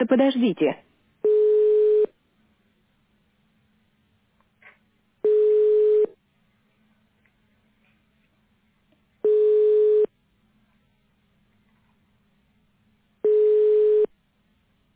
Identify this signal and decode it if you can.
Ringback tone (425 Hz, 1 s on, 3 s off), with speech.
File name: 1040.3000.80aa3771-292e-490f-9e68